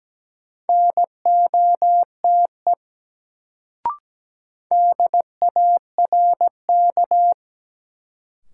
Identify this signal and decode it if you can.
Morse code, heard as a keyed tone, about 17 wpm, 700 Hz. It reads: NOTE DARK
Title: Sentences from Top 500 Words - New Word Drill - 17wpm